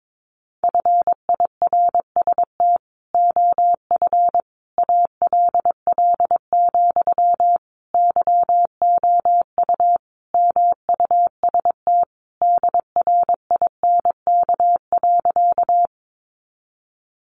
Morse code, heard as a keyed tone, about 22 wpm, 700 Hz. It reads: FIRST OF ALL, YOU MUST DRINK.
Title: Sentences from Top 1500 Words 22wpm